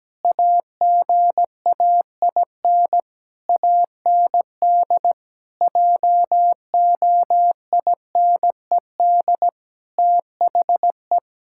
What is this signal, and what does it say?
Morse code, heard as a keyed tone, about 17 wpm, 700 Hz. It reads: AGAIN AND JOINED THE
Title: Pangram Challenge v2 - 17wpm